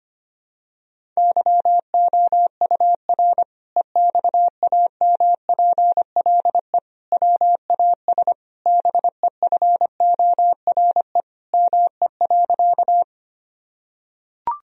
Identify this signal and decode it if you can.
Morse code, heard as a keyed tone, about 25 wpm, 700 Hz. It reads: YOUR EXAMPLE WAS BEFORE ME.
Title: Sentences from Top 1000 Words - Review Part 1 - 25wpm